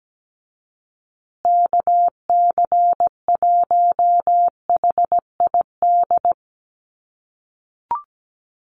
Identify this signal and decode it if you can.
Morse code, heard as a keyed tone, about 17 wpm, 700 Hz. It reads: KC1HID
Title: USA Call Signs Repeated 3x 17wpm